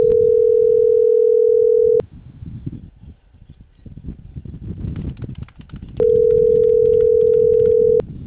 Ringback tone (North America), 440 + 480 Hz.